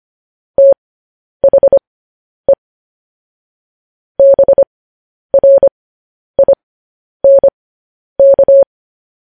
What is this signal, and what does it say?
Morse code, heard as a keyed tone, about 25 wpm, spaced as for 10 wpm, 550 Hz. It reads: THE BRINK
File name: podcast / News headlines-1x-25x10